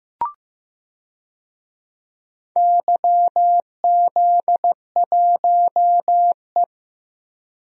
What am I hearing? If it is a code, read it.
Morse code, heard as a keyed tone, about 15 wpm, 700 Hz. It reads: YZ1E
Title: DXCC-100 Call Signs - 4 Characters - Repeated 3x - 15wpm